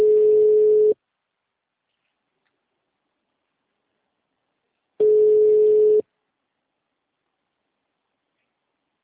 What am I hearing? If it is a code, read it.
Ringback tone (Europe, 425 Hz).
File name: out- -2014- - - .164